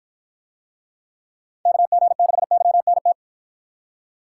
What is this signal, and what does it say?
Morse code, heard as a keyed tone, about 53 wpm, 700 Hz. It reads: KG6YNT